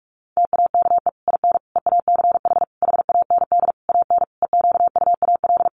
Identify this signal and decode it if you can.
Morse code, heard as a keyed tone, about 44 wpm, 700 Hz. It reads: TAKE IN EACH HAND AN EQUAL
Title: Sentences from Top 1400 Words 45wpm